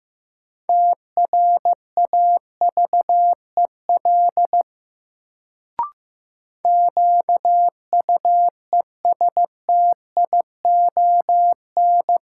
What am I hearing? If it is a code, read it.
Morse code, heard as a keyed tone, about 15 wpm, 700 Hz. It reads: TRAVEL QUESTION
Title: Sentences from Top 400 Words - New Word Drill - 15wpm